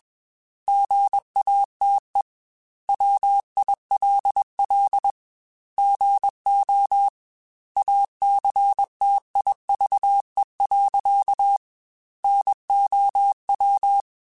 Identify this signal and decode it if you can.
Morse code, heard as a keyed tone, about 21 wpm, 800 Hz. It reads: GATE WILL GO ACTIVE. NOW
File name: Technical Subjects / Technical Subjects-mp3s / AND Gate @20